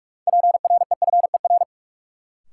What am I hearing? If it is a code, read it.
Morse code, heard as a keyed tone, about 46 wpm, 700 Hz. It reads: PREFER
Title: Sentences from Top 1100 Words - New Word Drill Alt - 45wpm